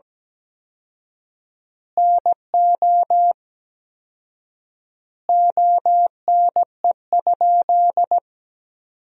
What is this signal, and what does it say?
Morse code, heard as a keyed tone, about 17 wpm, 700 Hz. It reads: NO ONE?
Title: 4x Word-Spacing - Sentences from Top 300 Words - 17wpm